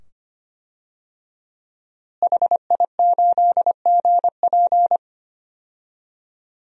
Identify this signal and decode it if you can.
Morse code, heard as a keyed tone, about 25 wpm, 700 Hz. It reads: HI8GP
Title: DXCC-100 Call Signs - 5 Characters - Repeated 2x - 25wpm